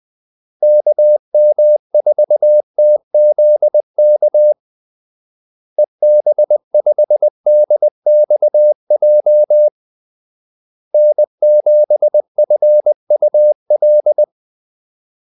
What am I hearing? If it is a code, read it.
Morse code, heard as a keyed tone, about 20 wpm, 600 Hz. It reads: KM4TZK EB5DXJ N7FUL
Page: Kaldesignaler 20 wpm